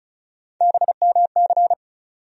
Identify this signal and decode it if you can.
Morse code, heard as a keyed tone, about 36 wpm, 700 Hz. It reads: BMC
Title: USA Call Sign Postfixes - 3 Letters 35wpm